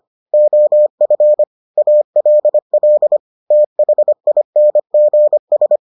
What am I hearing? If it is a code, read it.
Morse code, heard as a keyed tone, about 25 wpm, 600 Hz. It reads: OF ALL THINGS